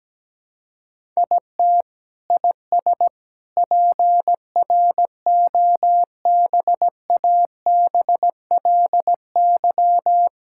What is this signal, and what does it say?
Morse code, heard as a keyed tone, about 17 wpm, 700 Hz. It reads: IT IS PROBABLY
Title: Sentences from Top 1000 Words - Review Part 6 - 17wpm